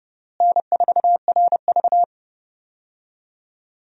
Morse code, heard as a keyed tone, about 32 wpm, 700 Hz. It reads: N4RV
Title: USA Call Signs - 4 Characters - Repeated 3x - 30wpm